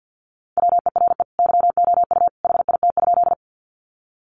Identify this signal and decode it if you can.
Morse code, heard as a keyed tone, about 50 wpm, 700 Hz. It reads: WERE YOU HIT?
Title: Sentences from Top 1200 Words 50wpm